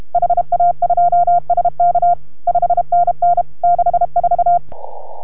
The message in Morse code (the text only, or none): HA2SK 5NN 64